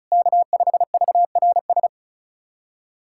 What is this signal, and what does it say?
Morse code, heard as a keyed tone, about 36 wpm, 690 Hz. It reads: K5VRS
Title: SOTA QSO Practice 35wpm